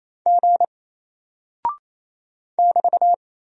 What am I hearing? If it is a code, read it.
Morse code, heard as a keyed tone, about 27 wpm, 700 Hz. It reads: G <BT>
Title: Single Character (Letter:Number:Punctuation:Prosign) Alt 28wpm